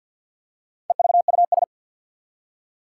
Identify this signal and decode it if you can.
Morse code, heard as a keyed tone, about 47 wpm, 700 Hz. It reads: EVUS
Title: Four Challenging Letter-Number Combinations 50wpm